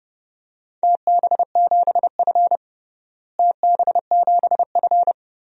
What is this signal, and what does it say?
Morse code, heard as a keyed tone, about 30 wpm, 700 Hz. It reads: TB7F TB7F
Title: DXCC-100 Call Signs - 4 Characters - Repeated 3x - 30wpm